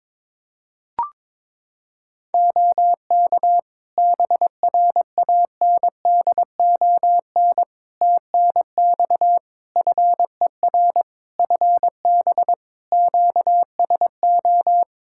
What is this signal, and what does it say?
Morse code, heard as a keyed tone, about 22 wpm, 700 Hz. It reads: OK BRANDON TNX FER FB QSO
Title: K1USN SST QSO Practice 22wpm